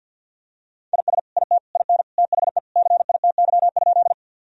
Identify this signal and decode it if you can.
Morse code, heard as a keyed tone, about 51 wpm, 700 Hz. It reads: IS IT IN THE CITY?
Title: Sentences from Top 1000 Words - Review Part 1 - 50wpm